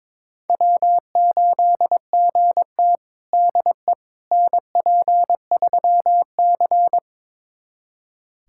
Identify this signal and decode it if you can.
Morse code, heard as a keyed tone, about 22 wpm, 700 Hz. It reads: W8GT DE NP3C
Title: USA Callsign DE Callsign - 4 Characters - 22wpm